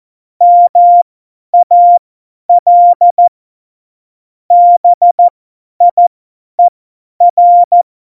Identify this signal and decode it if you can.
Morse code, heard as a keyed tone, about 14 wpm, spaced as for 10 wpm, 700 Hz. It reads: MAL BIER